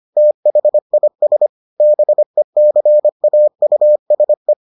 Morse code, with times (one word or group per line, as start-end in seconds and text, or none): 0.2-1.5 THIS
1.8-4.5 BECAUSE